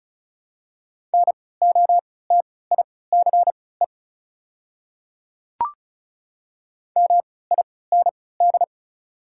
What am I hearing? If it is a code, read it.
Morse code, heard as a keyed tone, about 34 wpm, spaced as for 20 wpm, 690 Hz. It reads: NOTICE MIND
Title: 3x Character Spacing - Top 500 Words - 35wpm